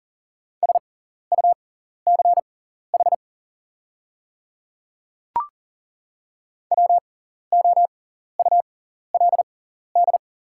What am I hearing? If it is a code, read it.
Morse code, heard as a keyed tone, about 40 wpm, spaced as for 14 wpm, 700 Hz. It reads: SUCH WOULD
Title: 6x Character Spacing - Top 200 Words - 40wpm